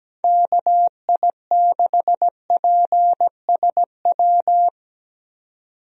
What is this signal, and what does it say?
Morse code, heard as a keyed tone, about 17 wpm, 700 Hz. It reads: KI6PSW
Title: USA Call Signs - 6 Characters - 17wpm